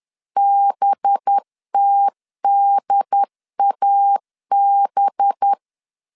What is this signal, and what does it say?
Morse code, heard as a keyed tone, about 10 wpm, 800 Hz. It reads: BTDAB
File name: LEZIONI / 8 YJ45 / 8 Y 10wpm 2615